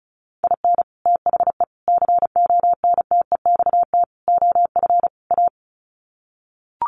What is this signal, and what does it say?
Morse code, heard as a keyed tone, about 34 wpm, 700 Hz. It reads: IN THE CONTEXT OF A
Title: Sets of 5 Words Encore 35wpm